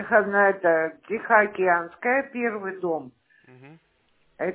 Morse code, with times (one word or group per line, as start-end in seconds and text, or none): none